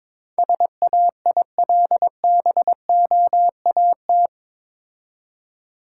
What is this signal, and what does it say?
Morse code, heard as a keyed tone, about 24 wpm, 700 Hz. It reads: SAILBOAT
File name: Easy Words - Speed-Racing with Hints - 22wpm